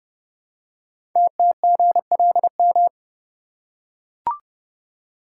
Morse code, heard as a keyed tone, about 30 wpm, 700 Hz. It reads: TTGLM
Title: Code Groups - 5 Letters - 30wpm